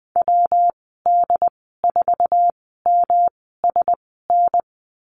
Morse code, heard as a keyed tone, about 20 wpm, spaced as for 14 wpm, 700 Hz. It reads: WD4MSN
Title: 147 Lesson 24g - Letter y - Call Signs 20wpm